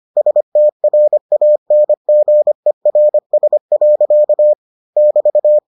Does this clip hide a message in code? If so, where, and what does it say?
Yes: Morse code from 0.2 to 5.6 s, heard as a keyed tone, about 25 wpm, 600 Hz. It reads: STRANGERS. <BT>